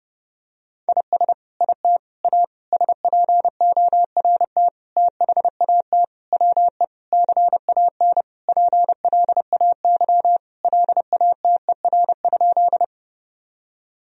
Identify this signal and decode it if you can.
Morse code, heard as a keyed tone, about 30 wpm, 700 Hz. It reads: IS IT A SPORT THAT WE CAN PLAY LATER?
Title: Sentences from Top 1000 Words - Review Part 4 - 30wpm